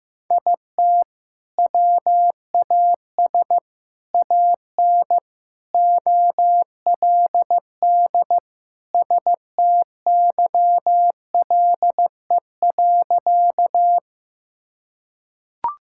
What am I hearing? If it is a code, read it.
Morse code, heard as a keyed tone, about 16 wpm, 700 Hz. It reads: IT WAS AN OLD STYLE.
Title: Sentences from Top 1000 Words - Review Part 5 - 15wpm